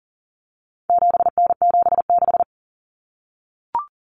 Morse code, heard as a keyed tone, about 39 wpm, 700 Hz. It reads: 7N76
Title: Common Cut Numbers - 4-Digit 40wpm